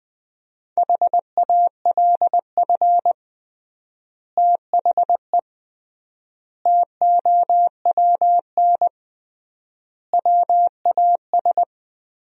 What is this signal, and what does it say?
Morse code, heard as a keyed tone, about 20 wpm, 700 Hz. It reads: HALF THE TOWN WAS
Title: 3x Word-Spacing - Sentences from Top 500 Words - 20wpm